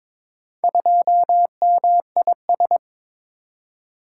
Morse code, heard as a keyed tone, about 22 wpm, 700 Hz. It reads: 2MIS
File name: practice / 22wpm / Four Letter-Number Combinations - 22wpm